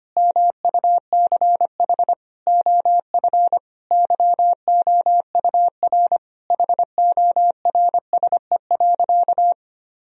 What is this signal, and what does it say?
Morse code, heard as a keyed tone, about 25 wpm, 700 Hz. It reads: MUCH OF YOUR HORSE.
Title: Sentences from Top 1000 Words - Review Part 2 - 25wpm